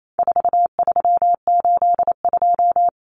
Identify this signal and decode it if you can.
Morse code, heard as a keyed tone, about 28 wpm, 700 Hz. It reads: 4382
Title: Common Cut Numbers - 4-Digit 28wpm